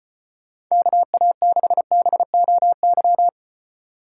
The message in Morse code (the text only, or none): KA6BOY